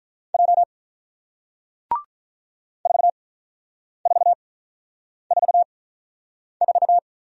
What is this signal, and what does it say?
Morse code, heard as a keyed tone, about 48 wpm, spaced as for 9 wpm, 700 Hz. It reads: J 4444